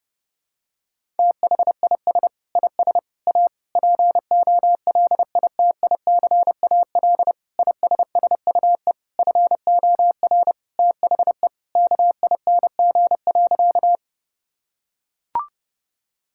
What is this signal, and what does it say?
Morse code, heard as a keyed tone, about 28 wpm, 700 Hz. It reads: THIS IS A POLITICAL ISSUE FOR THE KING.
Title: Sentences from Top 1000 Words - Review Part 6 - 30wpm